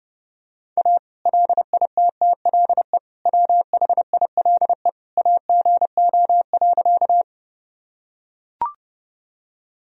Morse code, heard as a keyed tone, about 32 wpm, 700 Hz. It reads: A LITTLE WHILE AGO.
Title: Sentences from Top 1000 Words - Review Part 3 - 30wpm